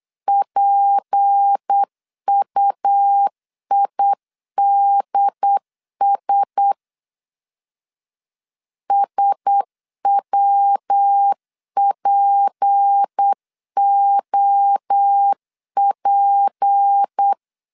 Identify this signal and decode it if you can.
Morse code, heard as a keyed tone, about 8 wpm, 800 Hz. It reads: PUIDS SWPOP
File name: LEZIONI / 6 PKX0 / 6 P 8wpm 3025